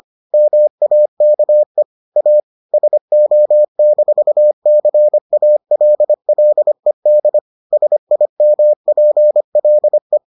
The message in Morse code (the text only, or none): MAKE A SO-CALLED SIMPLE